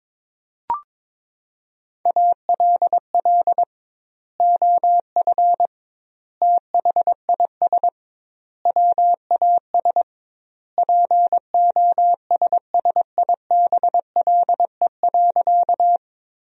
Morse code, heard as a keyed tone, about 22 wpm, 700 Hz. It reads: ALL OF THIS WAS POSSIBLE.
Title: 2x Word-Spacing - Sentences from Top 500 Words - 22wpm